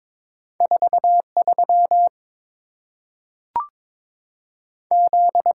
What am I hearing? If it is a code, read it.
Morse code, heard as a keyed tone, about 22 wpm, 700 Hz. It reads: 4 3 Z
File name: practice / 22wpm / Two Letter-Number Combinations - 22wpm